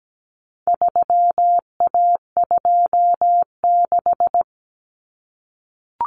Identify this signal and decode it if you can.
Morse code, heard as a keyed tone, about 17 wpm, 700 Hz. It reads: 3A26
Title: Common Cut Numbers - 4-Digit 17wpm